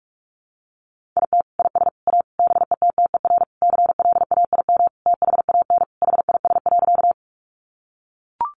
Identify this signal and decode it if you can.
Morse code, heard as a keyed tone, about 44 wpm, 700 Hz. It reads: IT IS A BETTER CLAIM THAN HIS.
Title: Sentences from Top 1500 Words 45wpm